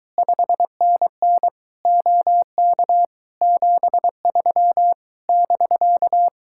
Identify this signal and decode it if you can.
Morse code, heard as a keyed tone, about 24 wpm, 700 Hz. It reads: 5NN OK 73 <BK>